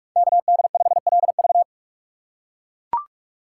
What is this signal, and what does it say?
Morse code, heard as a keyed tone, about 43 wpm, 700 Hz. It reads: KD5LV
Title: USA Call Signs - 5 Characters - 3x Character Spacing - 45wpm